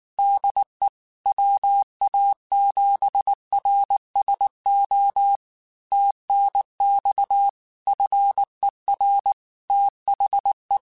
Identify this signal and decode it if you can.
Morse code, heard as a keyed tone, about 19 wpm, 800 Hz. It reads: DE WA7RSO TNX FER THE
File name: QSOs / QSOs-mp3s / kc7sac@18